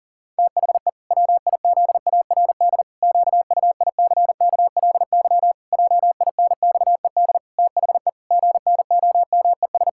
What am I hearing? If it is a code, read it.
Morse code, heard as a keyed tone, about 40 wpm, 700 Hz. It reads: THE WIZARD QUICKLY JINXED THE GNOMES
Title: Pangram Challenge v2 - 40wpm